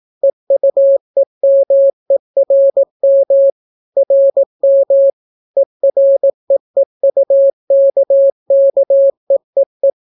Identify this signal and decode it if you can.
Morse code, heard as a keyed tone, about 18 wpm, 550 Hz. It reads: EUEMERM RM EREEUKKEEE